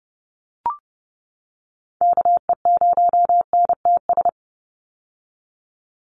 Morse code, heard as a keyed tone, about 30 wpm, 700 Hz. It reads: KE0NTS
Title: USA Call Signs 30wpm